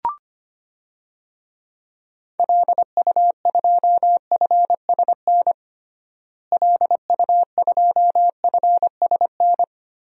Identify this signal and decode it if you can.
Morse code, heard as a keyed tone, about 25 wpm, 700 Hz. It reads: LU2FSN LU2FSN
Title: DXCC-100 Call Signs - 6 Characters - Repeated 3x - 25wpm